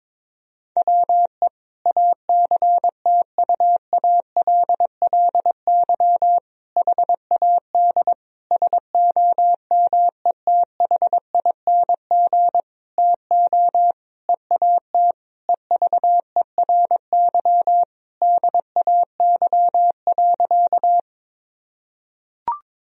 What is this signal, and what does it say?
Morse code, heard as a keyed tone, about 22 wpm, 700 Hz. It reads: WE ACTUALLY HAD SOMETHING TO EAT EVERY DAY.
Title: Sentences from Top 1000 Words - Review Part 4 - 22wpm